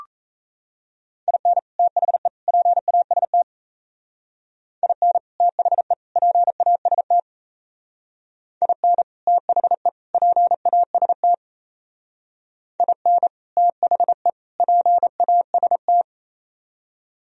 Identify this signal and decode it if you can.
Morse code, heard as a keyed tone, about 37 wpm, 700 Hz. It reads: IN THE PAST IN THE PAST IN THE PAST IN THE PAST